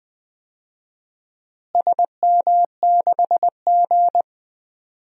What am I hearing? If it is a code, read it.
Morse code, heard as a keyed tone, about 20 wpm, 700 Hz. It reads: SM6G
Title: DXCC-100 Call Signs - 4 Characters - Repeated 3x - 20wpm